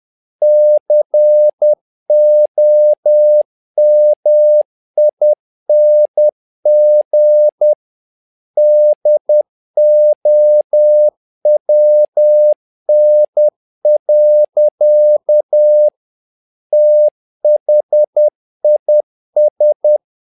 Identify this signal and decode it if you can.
Morse code, heard as a keyed tone, about 10 wpm, 600 Hz. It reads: COMING DOWN. THIS
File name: learning center / tower trouble / tower trouble 10wpm